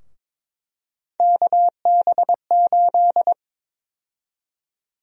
Morse code, heard as a keyed tone, about 22 wpm, 700 Hz. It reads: KB8
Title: USA Call Sign Prefixes - Repeated 2x - 22wpm